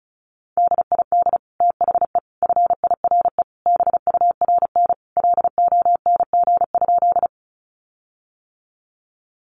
Morse code, heard as a keyed tone, about 34 wpm, 700 Hz. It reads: DID THE FIRE BURN LONG?
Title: Sentences from Top 1300 Words 35wpm